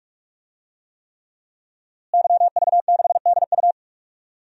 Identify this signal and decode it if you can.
Morse code, heard as a keyed tone, about 46 wpm, 700 Hz. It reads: YV6DU
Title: DXCC-100 Call Signs - 5 Characters - Repeated 3x - 45wpm